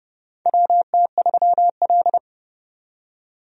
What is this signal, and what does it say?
Morse code, heard as a keyed tone, about 31 wpm, 700 Hz. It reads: WT3L
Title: USA Call Signs - 4 Characters - 30wpm